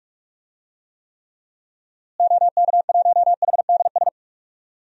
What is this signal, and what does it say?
Morse code, heard as a keyed tone, about 41 wpm, 700 Hz. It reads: OK1HDS
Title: DXCC-100 Call Signs - 6 Characters - Repeated 3x - 45wpm